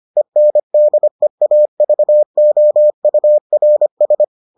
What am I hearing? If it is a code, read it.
Morse code, heard as a keyed tone, about 25 wpm, 600 Hz. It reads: ENDEAVOURS